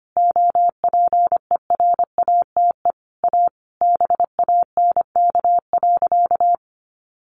Morse code, heard as a keyed tone, about 25 wpm, 700 Hz. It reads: OPERATE A BANK.
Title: Sentences from Top 1500 Words 25wpm